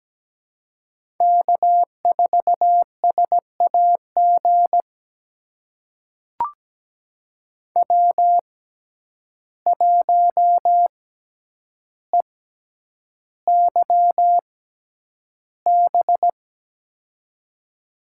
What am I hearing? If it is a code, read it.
Morse code, heard as a keyed tone, about 17 wpm, spaced as for 9 wpm, 700 Hz. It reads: K4SAG W1EYB
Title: USA Call Signs - 6x Character Spacing - 17wpm